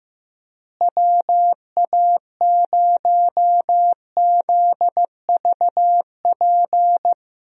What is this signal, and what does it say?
Morse code, heard as a keyed tone, about 15 wpm, 700 Hz. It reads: WA0ZVP
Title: USA Call Signs - 6 Characters - Repeated 2x - 15wpm